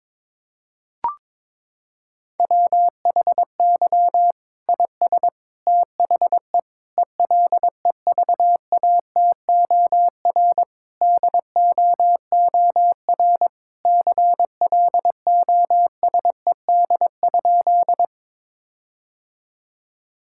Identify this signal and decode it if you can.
Morse code, heard as a keyed tone, about 21 wpm, 700 Hz. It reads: WHY IS THE ELEVATOR DOOR CLOSED?
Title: Sentences from Top 1600 Words 22wpm